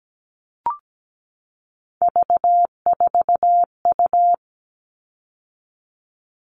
Morse code, heard as a keyed tone, about 17 wpm, 700 Hz. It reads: V4U
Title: All Cut Numbers - 3-Digit 17wpm